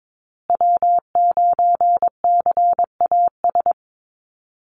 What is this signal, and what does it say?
Morse code, heard as a keyed tone, about 22 wpm, 700 Hz. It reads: W9CAS
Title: USA Call Signs Repeated 2x 22wpm